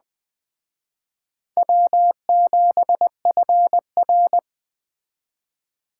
Morse code, heard as a keyed tone, about 20 wpm, 700 Hz. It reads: W7FR
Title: Speed Racing - USA Call Signs 17wpm